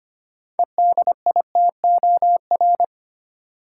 Morse code, heard as a keyed tone, about 25 wpm, 700 Hz. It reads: EDITOR
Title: Six Letter Words - 25wpm